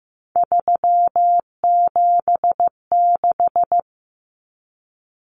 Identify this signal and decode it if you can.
Morse code, heard as a keyed tone, about 15 wpm, 700 Hz. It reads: 376
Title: Common Cut Numbers - 3-Digit 15wpm